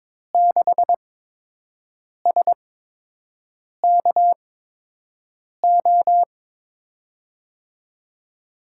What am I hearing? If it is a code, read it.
Morse code, heard as a keyed tone, about 22 wpm, spaced as for 6 wpm, 700 Hz. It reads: 6SKO
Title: USA Call Signs - 6 Characters - 8x Character Spacing - 22wpm